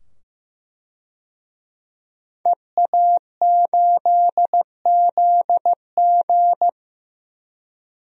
Morse code, heard as a keyed tone, about 15 wpm, 700 Hz. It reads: EA8ZG
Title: DXCC-100 Call Signs - 5 Characters - 15wpm